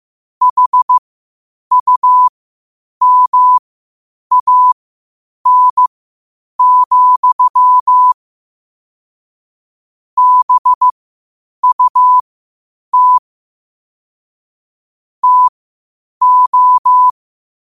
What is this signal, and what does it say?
Morse code, heard as a keyed tone, about 15 wpm, spaced as for 8 wpm, 1000 Hz. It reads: HUMAN, BUT TO
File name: qotdfiles / 8wpm